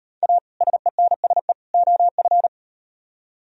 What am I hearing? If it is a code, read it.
Morse code, heard as a keyed tone, about 38 wpm, 700 Hz. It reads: A SENSE OF